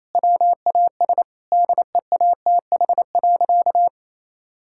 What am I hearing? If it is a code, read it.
Morse code, heard as a keyed tone, about 27 wpm, 700 Hz. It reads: WAS DEATH.
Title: Sentences from Top 1600 Words 28wpm